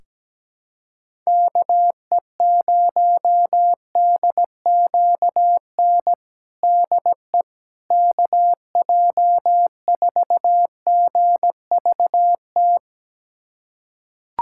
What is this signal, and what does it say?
Morse code, heard as a keyed tone, about 17 wpm, 700 Hz. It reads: KE0DQN DE KJ4GVT
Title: USA Callsign DE Callsign - 6 Characters - 17wpm